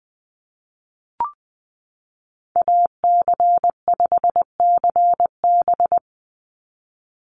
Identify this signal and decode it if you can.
Morse code, heard as a keyed tone, about 20 wpm, 700 Hz. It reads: AC5CB
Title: USA Call Signs 20wpm